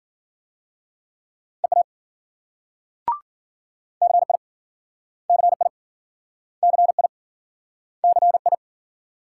Morse code, heard as a keyed tone, about 53 wpm, 700 Hz. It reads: EA CI CI CI CI